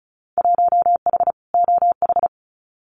Morse code, heard as a keyed tone, about 36 wpm, 700 Hz. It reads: 1H OH
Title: Winter Field Day Exchanges 35wpm